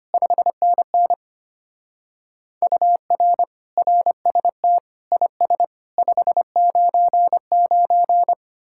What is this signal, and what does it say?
Morse code, heard as a keyed tone, about 32 wpm, 700 Hz. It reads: 5NN UR RST IS 599